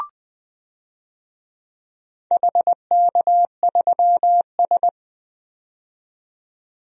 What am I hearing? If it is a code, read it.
Morse code, heard as a keyed tone, about 20 wpm, 700 Hz. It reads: HK3S
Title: DXCC-100 Call Signs - 4 Characters - 20wpm